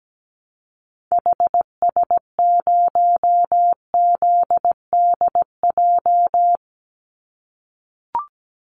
Morse code, heard as a keyed tone, about 17 wpm, 700 Hz. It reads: HS0ZDJ
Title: Call Signs - 17wpm